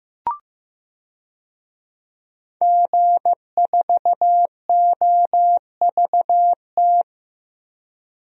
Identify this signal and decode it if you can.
Morse code, heard as a keyed tone, about 15 wpm, 700 Hz. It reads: G4OVT
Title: DXCC-100 Call Signs - 5 Characters - Repeated 3x - 15wpm